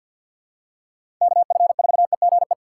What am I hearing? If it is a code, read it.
Morse code, heard as a keyed tone, about 54 wpm, 700 Hz. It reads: KF4EGE
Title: USA Call Signs - 6 Characters - 3x Character Spacing - 50wpm